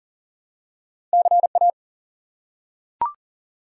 Morse code, heard as a keyed tone, about 42 wpm, 700 Hz. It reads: CA
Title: US States and Canadian Provinces 40wpm